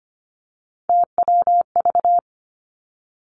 Morse code, heard as a keyed tone, about 25 wpm, 700 Hz. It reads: TWV
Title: Three Challenging Letter-Number Combinations 25wpm